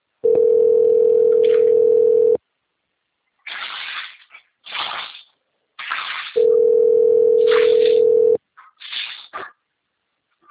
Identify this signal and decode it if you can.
Ringback tone (North America), 440 + 480 Hz.